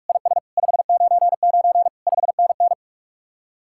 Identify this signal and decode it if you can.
Morse code, heard as a keyed tone, about 58 wpm, 700 Hz. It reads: IS 599 5NN